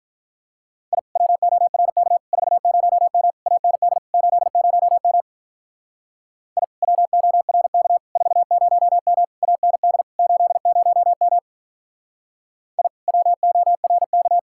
Morse code, heard as a keyed tone, about 43 wpm, 700 Hz. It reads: I WORK 40M AND 80M I WORK 40M AND 80M I WORK
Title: Speed-Racing Common QSO Phrases - 35wpm